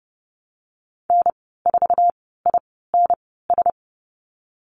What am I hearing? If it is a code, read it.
Morse code, heard as a keyed tone, about 30 wpm, spaced as for 17 wpm, 700 Hz. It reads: N4INS
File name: Lesson 14g - Number 2 - Call Signs 30wpm